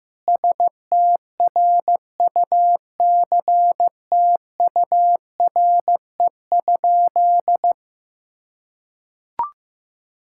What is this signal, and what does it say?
Morse code, heard as a keyed tone, about 15 wpm, 700 Hz. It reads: STRUCTURE?
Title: Sentences from Top 1000 Words - Review Part 6 - 15wpm